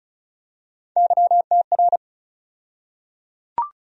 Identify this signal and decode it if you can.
Morse code, heard as a keyed tone, about 36 wpm, 700 Hz. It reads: YTR
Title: USA Call Sign Postfixes - 3 Letters 35wpm